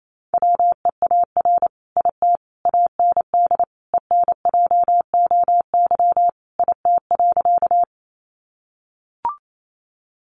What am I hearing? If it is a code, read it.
Morse code, heard as a keyed tone, about 28 wpm, 700 Hz. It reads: WEAR IT AND ENJOY IT.
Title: Sentences from Top 1500 Words 28wpm